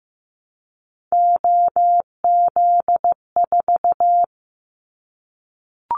Morse code, heard as a keyed tone, about 16 wpm, 700 Hz. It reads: OZ4
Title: Callsign Prefixes for Top 100 DXCC - 15wpm